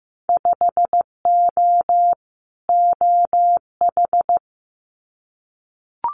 Morse code, heard as a keyed tone, about 15 wpm, 700 Hz. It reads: 5O OH
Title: Winter Field Day Exchanges Encore 15wpm